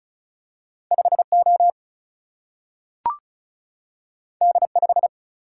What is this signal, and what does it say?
Morse code, heard as a keyed tone, about 36 wpm, 700 Hz. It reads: 5O D5
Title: Single-Letter Single-Number 35wpm